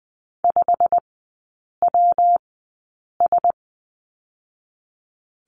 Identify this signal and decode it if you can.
Morse code, heard as a keyed tone, about 20 wpm, 700 Hz. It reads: 5 W S
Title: 114 Lesson 19e - Letter w - Three Characters 20wpm